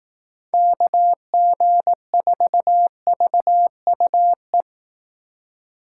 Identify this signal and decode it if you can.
Morse code, heard as a keyed tone, about 18 wpm, 700 Hz. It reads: KG4VUE
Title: Speed Racing - USA 2x3 Call Signs 15wpm